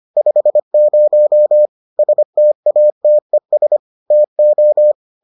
Morse code, heard as a keyed tone, about 25 wpm, 600 Hz. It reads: 50 STATES TO